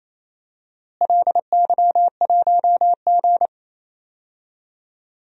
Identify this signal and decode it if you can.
Morse code, heard as a keyed tone, about 27 wpm, 700 Hz. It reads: LY1G
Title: DXCC-100 Call Signs - 4 Characters - Repeated 3x - 28wpm